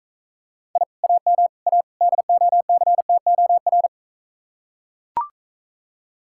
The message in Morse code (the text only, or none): I AM A DOCTOR